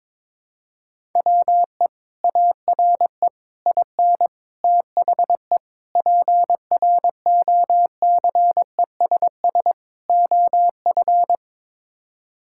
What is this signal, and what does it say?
Morse code, heard as a keyed tone, about 21 wpm, 700 Hz. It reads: WE ARE IN THE PROCESS OF
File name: practice / 22wpm / Sets of 6 Words - 22wpm